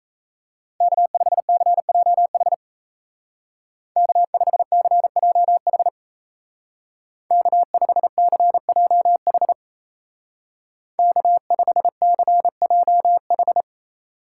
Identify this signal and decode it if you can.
Morse code, heard as a keyed tone, about 36 wpm, 700 Hz. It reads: K5CJH K5CJH K5CJH K5CJH